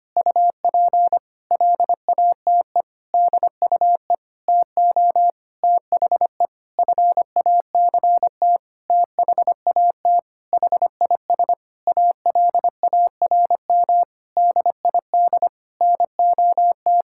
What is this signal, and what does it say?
Morse code, heard as a keyed tone, about 25 wpm, 700 Hz. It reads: UP LATE DUE TO THE FACT THAT HIS ALARM DID NOT
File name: practice / 25wpm / Sentences with 4-Word Phrases - 25wpm